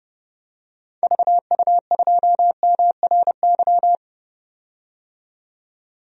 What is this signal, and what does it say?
Morse code, heard as a keyed tone, about 30 wpm, 700 Hz. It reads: VU2MRY
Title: DXCC-100 Call Signs - 6 Characters - 30wpm